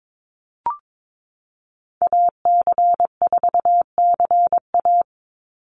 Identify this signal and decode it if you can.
Morse code, heard as a keyed tone, about 22 wpm, 700 Hz. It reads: AC4CA
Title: Call Signs Repeated 3x - 22wpm